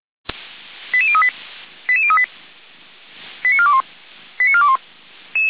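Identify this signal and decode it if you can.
Five-tone selective call (ZVEI), heard as quick runs of five tones, tones of about 70 ms. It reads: 88038 88038 89421 89421
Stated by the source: Five-tone ZVEI call received on an SDR, fire brigade Feuerwehr Orlamünde.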